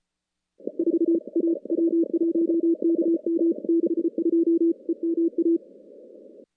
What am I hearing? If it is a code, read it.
Morse code, heard as a keyed tone, about 35 wpm, 330 Hz. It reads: 4 A W1KM 62 EMA
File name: contests / CW-SS-k5tr- -2300